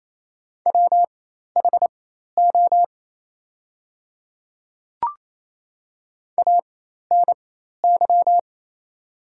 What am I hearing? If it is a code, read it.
Morse code, heard as a keyed tone, about 27 wpm, spaced as for 13 wpm, 700 Hz. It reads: WHO ANY
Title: 4x Character Spacing - Top 300 Words - 28wpm